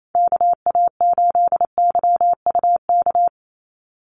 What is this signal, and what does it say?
Morse code, heard as a keyed tone, about 28 wpm, 700 Hz. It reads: KA8YUK
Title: USA Call Signs 28wpm